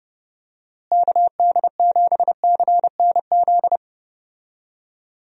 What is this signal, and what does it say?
Morse code, heard as a keyed tone, about 30 wpm, 700 Hz. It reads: KD7CNZ
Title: USA Call Signs - 6 Characters - Repeated 2x - 30wpm